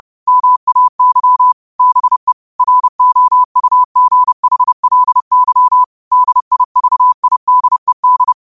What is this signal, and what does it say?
Morse code, heard as a keyed tone, about 30 wpm, 1000 Hz. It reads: MAY BE ROUGHLY DIVIDED
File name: qotdfiles / 30wpm